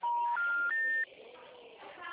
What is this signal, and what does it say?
Special information tone.